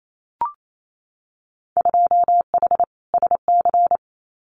Morse code, heard as a keyed tone, about 28 wpm, 700 Hz. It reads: 2H SC
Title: Winter Field Day Exchanges 28wpm